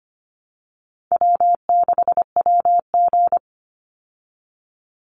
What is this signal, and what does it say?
Morse code, heard as a keyed tone, about 25 wpm, 700 Hz. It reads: W6WG
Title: Easy Call Signs Repeated 3x - 25wpm